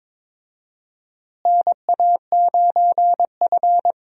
Morse code, heard as a keyed tone, about 22 wpm, 700 Hz. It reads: NA9F
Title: USA Call Signs - 4 Characters - Repeated 2x - 22wpm